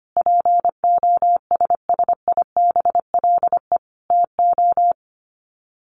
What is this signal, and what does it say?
Morse code, heard as a keyed tone, about 25 wpm, 700 Hz. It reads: POSSIBLE TO
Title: Sets of 5 Words Encore2 25wpm